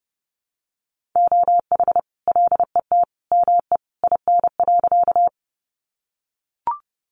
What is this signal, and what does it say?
Morse code, heard as a keyed tone, about 30 wpm, 700 Hz. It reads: OH LET ME IN.
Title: Sentences from Top 500 Words 30wpm